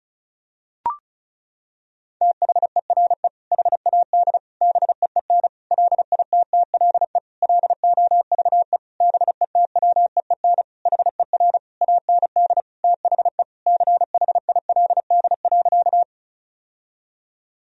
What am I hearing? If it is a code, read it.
Morse code, heard as a keyed tone, about 34 wpm, 700 Hz. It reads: THERE HAD BEEN LITTLE LOVE BETWEEN HER AND THE CHILD.
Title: Sentences from Top 1000 Words - Review Part 1 - 35wpm